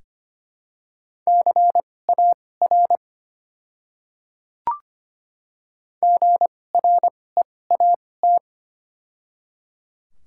Morse code, heard as a keyed tone, about 25 wpm, spaced as for 18 wpm, 700 Hz. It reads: CAR GREAT
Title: 2x Character Spacing - Top 300 Words - 25wpm